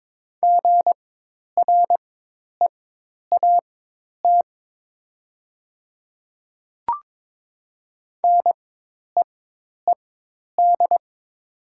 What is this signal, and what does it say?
Morse code, heard as a keyed tone, about 22 wpm, spaced as for 10 wpm, 700 Hz. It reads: GREAT NEED
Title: 4x Character Spacing - Top 300 Words - 22wpm